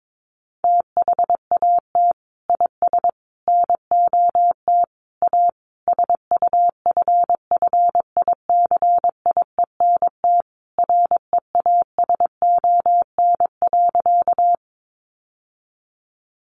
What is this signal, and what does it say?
Morse code, heard as a keyed tone, about 22 wpm, 700 Hz. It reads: THAT IS NOT A SUFFICIENT REASON.
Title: Sentences from Top 1400 Words 22wpm